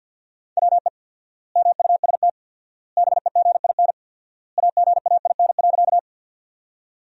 Morse code, heard as a keyed tone, about 48 wpm, 700 Hz. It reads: WE MUST BEGIN AGAIN.
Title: 4x Word-Spacing - Sentences from Top 300 Words - 50wpm